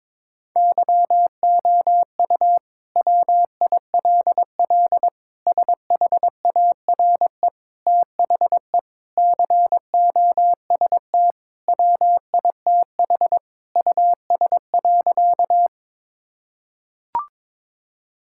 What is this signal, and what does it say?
Morse code, heard as a keyed tone, about 22 wpm, 700 Hz. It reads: YOU WILL SHARE THE COST WITH US.
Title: Sentences from Top 1000 Words - Review Part 6 - 22wpm